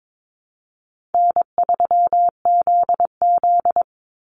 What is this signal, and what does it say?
Morse code, heard as a keyed tone, about 22 wpm, 700 Hz. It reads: N3ZZ
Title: Easy Call Signs Repeated 3x - 22wpm